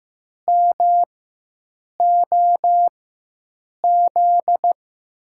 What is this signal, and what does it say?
Morse code, heard as a keyed tone, about 15 wpm, spaced as for 7 wpm, 700 Hz. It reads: MOZ
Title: USA Call Signs - 6 Characters - 4x Character Spacing - 15wpm